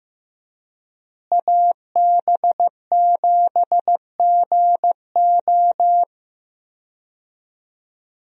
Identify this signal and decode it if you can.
Morse code, heard as a keyed tone, about 15 wpm, 700 Hz. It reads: AB7GO
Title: USA Call Signs - 5 Characters - Repeated 2x - 15wpm